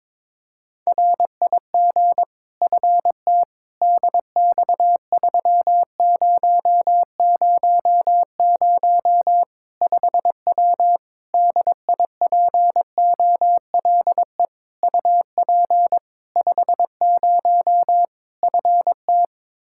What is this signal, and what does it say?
Morse code, heard as a keyed tone, about 22 wpm, 700 Hz. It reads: RIG FT DX3000 5W DIPOLE UP 50 FT
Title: Practice QSO - Exchange 2 - Info - 22wpm